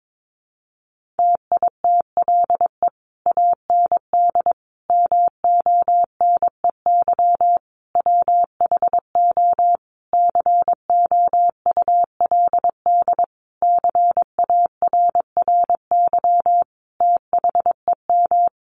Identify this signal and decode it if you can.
Morse code, heard as a keyed tone, about 22 wpm, 700 Hz. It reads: TITLE AND MONEY WHO COULD CARRY THEM
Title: Sentences from Top 1200 Words 22wpm